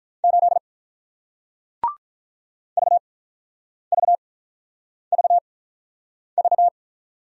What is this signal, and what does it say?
Morse code, heard as a keyed tone, about 48 wpm, spaced as for 9 wpm, 700 Hz. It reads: 8 VVVV